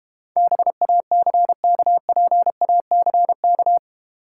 Morse code, heard as a keyed tone, about 31 wpm, 700 Hz. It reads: BACKPACK
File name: Easy Words - Speed-Racing with Hints - 25wpm